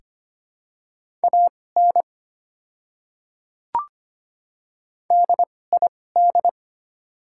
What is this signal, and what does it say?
Morse code, heard as a keyed tone, about 25 wpm, spaced as for 18 wpm, 690 Hz. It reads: AN DID
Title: 2x Character Spacing - Top 100 Words - 25wpm